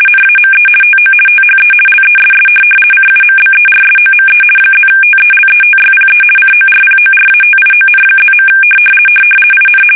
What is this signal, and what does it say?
RTTY, about 50 baud. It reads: NAME TO THE LESS CUMBERSOME +TELETYPE CORPORATION+. IN
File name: fsk850-50B / teletype-corp